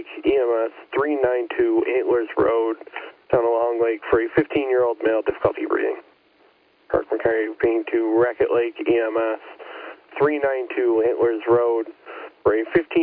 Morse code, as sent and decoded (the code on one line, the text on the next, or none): none